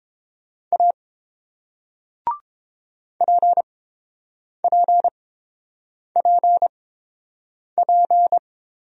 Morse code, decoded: A PPPP

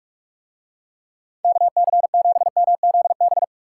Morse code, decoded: KC7MZB

